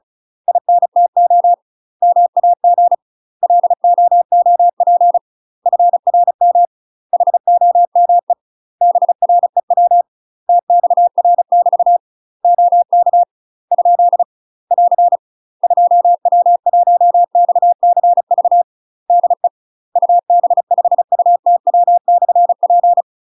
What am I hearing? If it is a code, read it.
Morse code, heard as a keyed tone, about 34 wpm, 700 Hz. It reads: INTO MAG LOOP FRM HOME BREW TXR <BT> OK ? <AR> 2W1XCV DE UB5UTW/P
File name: UK MASTER QSOs TRAINING / 35WPM / UK MASTER QSOs 0034